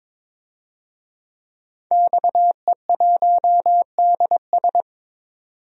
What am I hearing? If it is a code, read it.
Morse code, heard as a keyed tone, about 22 wpm, 700 Hz. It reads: XE1DS
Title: DXCC-100 Call Signs - 5 Characters - 22wpm